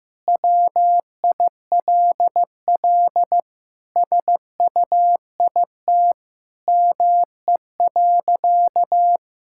Morse code, heard as a keyed tone, about 16 wpm, 700 Hz. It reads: WILL SUIT ME.
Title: Sentences from Top 1600 Words 15wpm